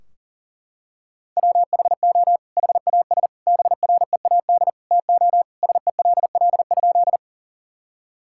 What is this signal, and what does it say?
Morse code, heard as a keyed tone, about 37 wpm, 700 Hz. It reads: WHO HAS BREAD TO SELL?
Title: Sentences from Top 1600 Words 40wpm